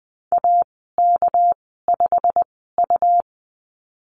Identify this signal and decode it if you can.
Morse code, heard as a keyed tone, about 20 wpm, spaced as for 14 wpm, 700 Hz. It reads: AK5U
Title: 200 Lesson 32g - Letter k - Call Signs 20wpm